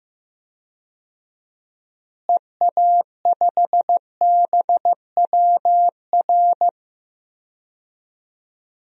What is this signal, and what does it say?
Morse code, heard as a keyed tone, about 15 wpm, 700 Hz. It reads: EA5BWR
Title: DXCC-100 Call Signs - 6 Characters - Repeated 3x - 15wpm